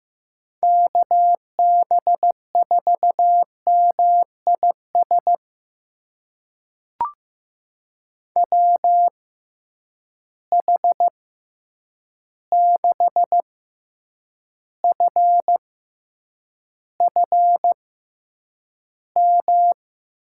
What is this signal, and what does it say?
Morse code, heard as a keyed tone, about 15 wpm, spaced as for 8 wpm, 700 Hz. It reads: KB4MIS WH6FFM
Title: USA Call Signs - 6 Characters - 6x Character Spacing - 15wpm